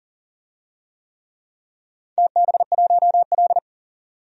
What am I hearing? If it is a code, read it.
Morse code, heard as a keyed tone, about 40 wpm, 700 Hz. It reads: TB1L
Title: DXCC-100 Call Signs - 4 Characters - 40wpm